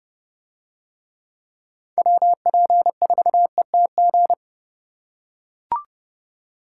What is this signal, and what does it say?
Morse code, heard as a keyed tone, about 29 wpm, 700 Hz. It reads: WP4ETG